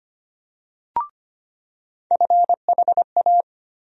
Morse code, heard as a keyed tone, about 25 wpm, 700 Hz. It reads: FHA
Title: USA Call Sign Postfixes - 3 Letters 25wpm